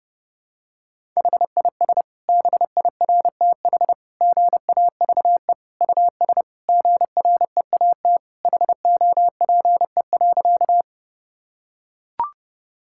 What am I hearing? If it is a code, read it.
Morse code, heard as a keyed tone, about 29 wpm, 700 Hz. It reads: HIS BIRTH GAVE US GREAT HOPE.
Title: Sentences from Top 1000 Words - Review Part 5 - 30wpm